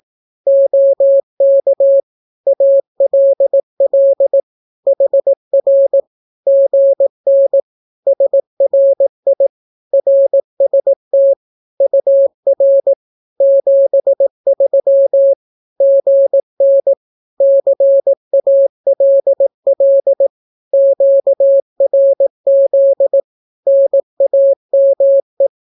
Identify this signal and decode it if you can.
Morse code, heard as a keyed tone, about 18 wpm, 550 Hz. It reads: OK ALL HR GN SRI RST UR 73 GN CALL QRZ NAME
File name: naukapodstaw / skroty / skroty